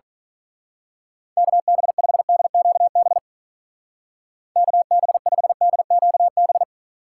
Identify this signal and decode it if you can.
Morse code, heard as a keyed tone, about 42 wpm, 700 Hz. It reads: KB5DQB KB5DQB
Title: Speed Racing - USA Call Signs 35wpm